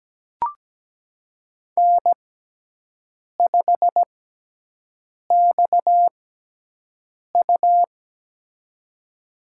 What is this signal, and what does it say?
Morse code, heard as a keyed tone, about 17 wpm, spaced as for 6 wpm, 700 Hz. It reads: N5XU
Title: USA Call Signs - 4 Characters - 6x Character Spacing - 17wpm